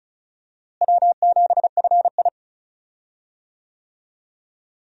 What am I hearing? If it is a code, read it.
Morse code, heard as a keyed tone, about 36 wpm, 700 Hz. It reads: W7FI
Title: USA Call Signs - 4 Characters - Repeated 3x - 35wpm